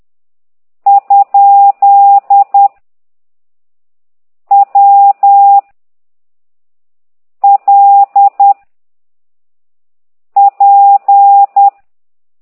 Morse code, heard as a keyed tone, about 10 wpm, 800 Hz. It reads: ? W L P
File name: ct2idw / exerc8